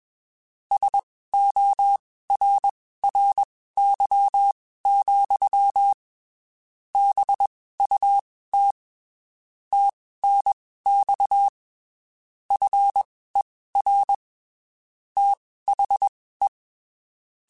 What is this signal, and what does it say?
Morse code, heard as a keyed tone, about 21 wpm, spaced as for 15 wpm, 800 Hz. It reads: SORRY, BUT TNX FER THE
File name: (spaced) / QSOs Spaced-mp3s / WA7QOX 2 @20 sp